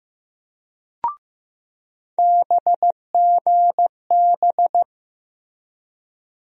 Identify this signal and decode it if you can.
Morse code, heard as a keyed tone, about 15 wpm, 700 Hz. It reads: BGB
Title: USA Call Sign Postfixes - 3 Letters 15wpm